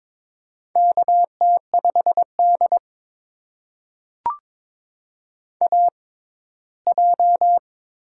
Morse code, heard as a keyed tone, about 23 wpm, spaced as for 15 wpm, 700 Hz. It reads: KT5D AJ
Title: USA Call Signs - 4 Characters - 6x Character Spacing - 22wpm